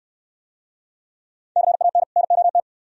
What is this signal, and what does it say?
Morse code, heard as a keyed tone, about 67 wpm, 700 Hz. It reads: CAN NOT